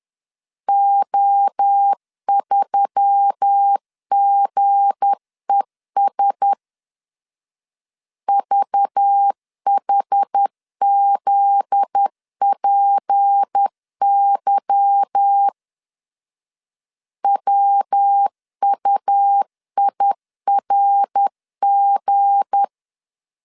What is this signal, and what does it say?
Morse code, heard as a keyed tone, about 10 wpm, 800 Hz. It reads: O3GES VHZPY WUIRG
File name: LEZIONI / 8 YJ45 / 8 Y 10wpm 2615